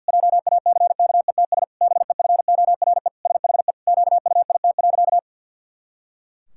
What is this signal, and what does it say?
Morse code, heard as a keyed tone, about 50 wpm, 700 Hz. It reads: JACKETS BEFORE SHE QUIT.
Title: Pangram Challenge v2 - 50wpm